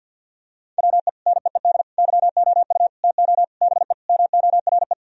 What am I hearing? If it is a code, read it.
Morse code, heard as a keyed tone, about 49 wpm, 700 Hz. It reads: WE NEED YOU TO BE MORE
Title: Sentences from Top 1000 Words - Review Part 3 - 50wpm